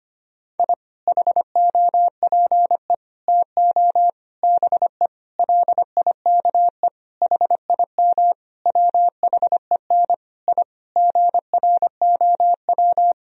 Morse code, heard as a keyed tone, about 25 wpm, 700 Hz. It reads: I HOPE TO BE LIKE HIM WHEN I GROW
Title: Sentences from Top 1000 Words - Review Part 1 - 25wpm